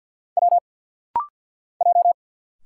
Morse code, heard as a keyed tone, about 50 wpm, 700 Hz. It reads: W J
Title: Single Letters - Rapid-Fire 50wpm